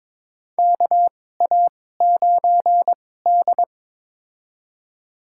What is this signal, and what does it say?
Morse code, heard as a keyed tone, about 21 wpm, spaced as for 16 wpm, 700 Hz. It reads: KA9D
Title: USA Call Signs - 4 Characters - 2x Character Spacing - 22wpm